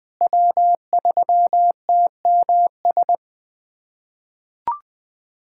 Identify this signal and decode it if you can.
Morse code, heard as a keyed tone, about 20 wpm, 700 Hz. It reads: W3TMS
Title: USA Callsign DE Callsign - 5 Characters - 20wpm